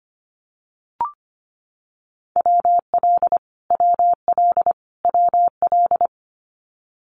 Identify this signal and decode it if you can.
Morse code, heard as a keyed tone, about 25 wpm, 700 Hz. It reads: WL WL WL
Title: Abbreviations Repeated 3x - 25wpm